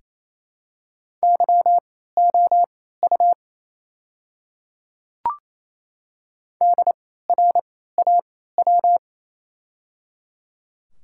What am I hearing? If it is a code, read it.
Morse code, heard as a keyed tone, about 27 wpm, spaced as for 16 wpm, 700 Hz. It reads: YOU DRAW
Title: 3x Character Spacing - Top 300 Words - 28wpm